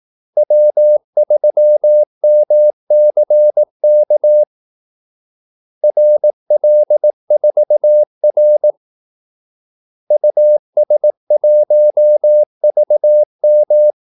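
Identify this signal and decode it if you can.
Morse code, heard as a keyed tone, about 18 wpm, 600 Hz. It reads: W3MCK RL4R US1VM
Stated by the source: Morse call-sign practice at 18 wpm.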